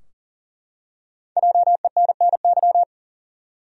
Morse code, heard as a keyed tone, about 40 wpm, 700 Hz. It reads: JENNY
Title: USA Common Names - 40wpm